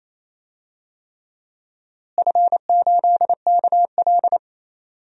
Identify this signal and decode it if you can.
Morse code, heard as a keyed tone, about 28 wpm, 700 Hz. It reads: F8KL
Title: DXCC-100 Call Signs - 4 Characters - Repeated 3x - 28wpm